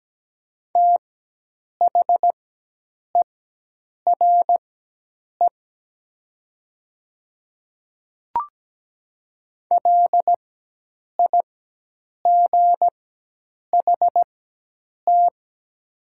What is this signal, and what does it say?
Morse code, heard as a keyed tone, about 17 wpm, spaced as for 8 wpm, 700 Hz. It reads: THERE LIGHT
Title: 4x Character Spacing - Top 200 Words - 17wpm